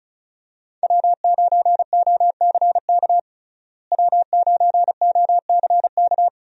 Morse code, heard as a keyed tone, about 34 wpm, 700 Hz. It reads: W9OCK W9OCK
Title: USA Call Signs - 5 Characters - Repeated 2x - 35wpm